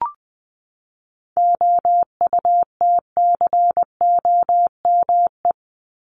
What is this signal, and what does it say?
Morse code, heard as a keyed tone, about 20 wpm, 700 Hz. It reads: OUTCOME
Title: Top 1501-1750 Words 20wpm